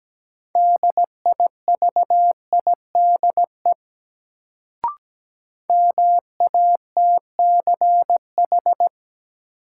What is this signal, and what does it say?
Morse code, heard as a keyed tone, about 17 wpm, 700 Hz. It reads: DIVIDE MATCH
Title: Sentences from Top 1600 Words - New Word Drill - 17wpm